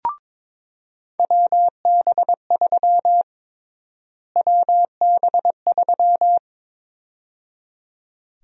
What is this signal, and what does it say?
Morse code, heard as a keyed tone, about 22 wpm, 700 Hz. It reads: WB3 WB3
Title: USA Call Sign Prefixes - Repeated 2x - 22wpm